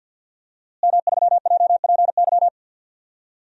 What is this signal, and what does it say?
Morse code, heard as a keyed tone, about 50 wpm, 700 Hz. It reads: M3JPY
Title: DXCC-100 Call Signs - 5 Characters - Repeated 3x - 50wpm